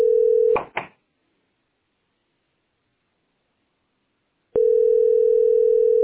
Ringback tone (North America), 440 + 480 Hz.